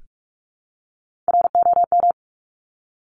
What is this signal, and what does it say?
Morse code, heard as a keyed tone, about 46 wpm, 700 Hz. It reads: ROM
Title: Top Three Letter Combinations in Words 45wpm